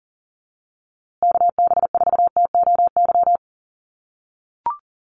Morse code, heard as a keyed tone, about 37 wpm, 700 Hz. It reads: KB4TOY
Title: USA Call Signs Repeated 2x 40wpm